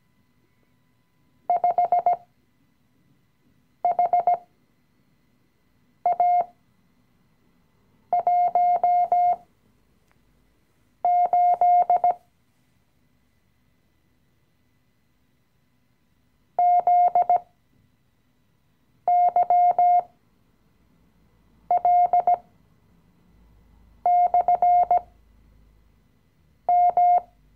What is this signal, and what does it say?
Morse code, heard as a keyed tone, about 17 wpm, spaced as for 5 wpm, 710 Hz. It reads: 5HA18 ZYL/M
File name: CW-Online-Kurs / Anfänger / Lektion 018